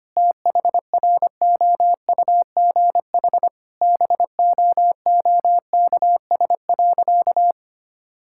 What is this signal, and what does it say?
Morse code, heard as a keyed tone, about 25 wpm, 700 Hz. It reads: THROUGH BOOKS.
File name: practice / 25wpm / Difficult Sentences with Hints - 25wpm